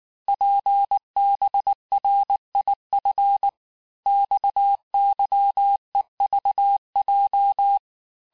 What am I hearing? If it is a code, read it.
Morse code, heard as a keyed tone, about 19 wpm, 800 Hz. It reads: PBRIF XYEVJ
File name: cours-du-lundi files / cours-1583--texte-chiffre-18-mpm